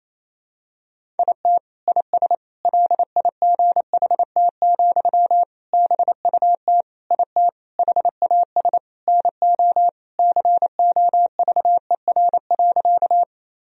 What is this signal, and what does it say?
Morse code, heard as a keyed tone, about 28 wpm, 700 Hz. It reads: IT IS LIGHT, BUT IT HAS NO COVER.
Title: Sentences from Top 1000 Words - Review Part 1 - 28wpm